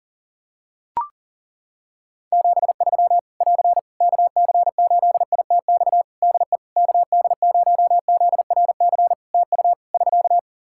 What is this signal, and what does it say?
Morse code, heard as a keyed tone, about 40 wpm, 700 Hz. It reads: 73 <AR> KC8ITX DE KD0ZRC TU <SK>
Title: Practice QSO - Exchange 4 - Ending - 40wpm